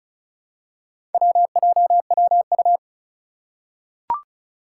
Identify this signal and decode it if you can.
Morse code, heard as a keyed tone, about 36 wpm, 700 Hz. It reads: WJWU
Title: Four Challenging Letter-Number Combinations 35wpm